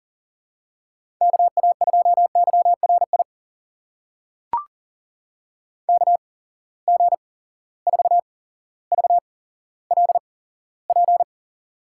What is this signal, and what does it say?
Morse code, heard as a keyed tone, about 39 wpm, spaced as for 17 wpm, 700 Hz. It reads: KA2YRI KG4VLP
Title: USA Call Signs - 6 Characters - 8x Character Spacing - 40wpm